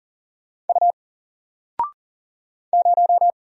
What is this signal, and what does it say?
Morse code, heard as a keyed tone, about 40 wpm, 700 Hz. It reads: U 0
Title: Single Letter-Number 40wpm